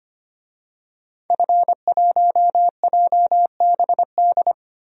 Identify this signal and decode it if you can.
Morse code, heard as a keyed tone, about 25 wpm, 700 Hz. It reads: F1JBD
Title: DXCC-100 Call Signs - 5 Characters - Repeated 3x - 25wpm